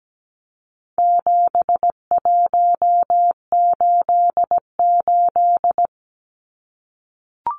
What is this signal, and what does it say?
Morse code, heard as a keyed tone, about 17 wpm, 700 Hz. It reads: 7188
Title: Numbers - 4-Digit 17wpm